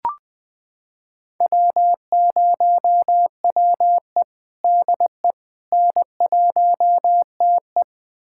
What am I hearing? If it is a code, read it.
Morse code, heard as a keyed tone, about 20 wpm, 700 Hz. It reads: W0WE DE N1TE
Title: USA Callsign DE Callsign - 4 Characters - 20wpm